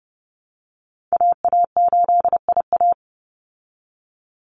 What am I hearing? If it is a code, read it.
Morse code, heard as a keyed tone, about 30 wpm, 700 Hz. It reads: AA8IA
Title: Call Signs Repeated 2x - 30wpm